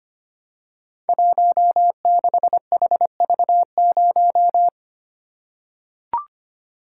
Morse code, heard as a keyed tone, about 25 wpm, 690 Hz. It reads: 16HV0
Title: Code Groups - 5 Letter:Numbers - 25wpm